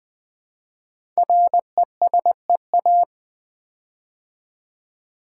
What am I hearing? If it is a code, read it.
Morse code, heard as a keyed tone, about 20 wpm, 700 Hz. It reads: RESEA
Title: Build A Word - 5 Letters At A Time - 10-Letter Words - 20wpm